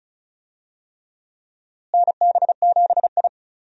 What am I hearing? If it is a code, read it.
Morse code, heard as a keyed tone, about 36 wpm, 700 Hz. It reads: NB7I
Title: USA Call Signs - 4 Characters - Repeated 3x - 35wpm